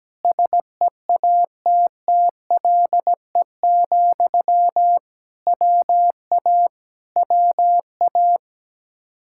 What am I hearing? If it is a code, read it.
Morse code, heard as a keyed tone, about 17 wpm, 700 Hz. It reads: SEATTLE, WA WA
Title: Practice QSO - Exchange 1 - Beginning - 17wpm